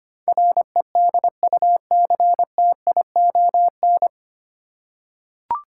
Morse code, heard as a keyed tone, about 25 wpm, 700 Hz. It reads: REDUCTION
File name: practice / 25wpm / 9 Letter Words - First 3 Letters Given - 25wpm